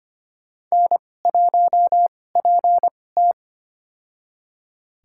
Morse code, heard as a keyed tone, about 25 wpm, spaced as for 18 wpm, 700 Hz. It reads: N1PT
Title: USA Call Signs - 2x Character Spacing - 25wpm